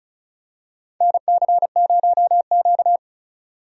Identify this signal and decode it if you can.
Morse code, heard as a keyed tone, about 34 wpm, 700 Hz. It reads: NC0Q
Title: USA Call Signs - 4 Characters - 35wpm